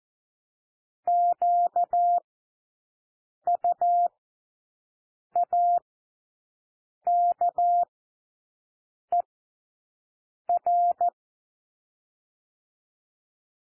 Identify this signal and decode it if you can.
Morse code, heard as a keyed tone, about 14 wpm, spaced as for 6 wpm, 700 Hz. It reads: QUAKER
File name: cwmp3 / W1 SOUND WORDS 14 wpm 0